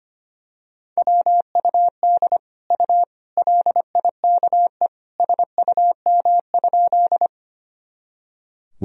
Morse code, heard as a keyed tone, about 25 wpm, 700 Hz. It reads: WUD U LIKE SUM?
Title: Sentences from Top 100 Words and Abbreviations 25wpm